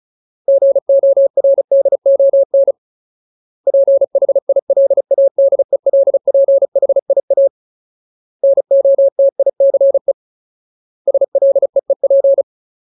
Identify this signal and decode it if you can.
Morse code, heard as a keyed tone, about 34 wpm, 550 Hz. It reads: GORDON PHILADELPHIA NOTICE SLEEP
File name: Ultimate Word List / 35WPM Ultimate